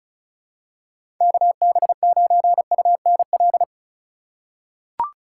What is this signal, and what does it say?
Morse code, heard as a keyed tone, about 34 wpm, 700 Hz. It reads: KB9UNL
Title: USA Call Signs - 6 Characters - 2x Character Spacing - 35wpm